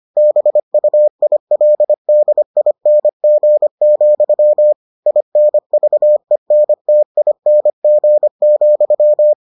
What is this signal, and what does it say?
Morse code, heard as a keyed tone, about 25 wpm, 600 Hz. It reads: BUILDING, INVENTING,